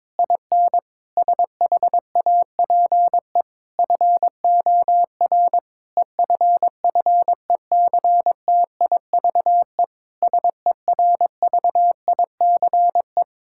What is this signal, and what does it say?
Morse code, heard as a keyed tone, about 22 wpm, 700 Hz. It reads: IN SHAPE FOR EFFECTIVE SERVICE
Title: Sentences from Top 1000 Words - Review Part 6 - 22wpm